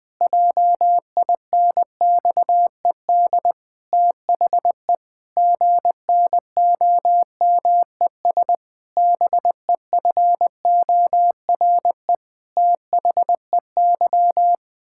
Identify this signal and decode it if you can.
Morse code, heard as a keyed tone, about 20 wpm, 700 Hz. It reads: JINXED THE GNOMES BEFORE THEY
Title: Pangram Challenge v2 - 20wpm